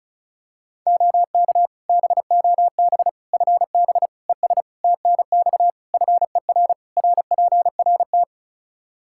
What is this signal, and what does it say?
Morse code, heard as a keyed tone, about 36 wpm, 700 Hz. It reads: OK BOB FB ES TNX FER RPRT
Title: Practice QSO - Exchange 2 - Info - 35wpm